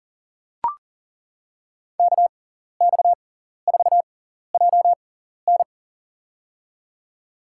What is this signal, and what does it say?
Morse code, heard as a keyed tone, about 40 wpm, spaced as for 14 wpm, 700 Hz. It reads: KX4JN
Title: USA Call Signs - 5 Characters - 6x Character Spacing - 40wpm